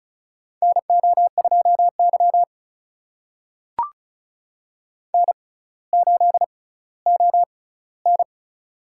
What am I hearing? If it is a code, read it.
Morse code, heard as a keyed tone, about 36 wpm, spaced as for 18 wpm, 700 Hz. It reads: NO2Y N8ON
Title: USA Call Signs - 6x Character Spacing - 35wpm